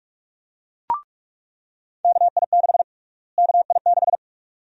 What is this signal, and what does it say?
Morse code, heard as a keyed tone, about 46 wpm, 700 Hz. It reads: KI6 KI6
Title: USA Call Sign Prefixes - Repeated 2x - 45wpm